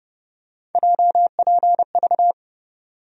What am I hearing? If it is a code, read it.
Morse code, heard as a keyed tone, about 30 wpm, 700 Hz. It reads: JPV
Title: USA Call Sign Postfixes - 3 Letters 30wpm